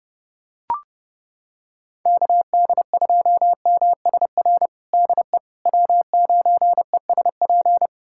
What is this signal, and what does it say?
Morse code, heard as a keyed tone, about 32 wpm, 700 Hz. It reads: KD2MSR DE W9ESP
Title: USA Callsign DE Callsign 30wpm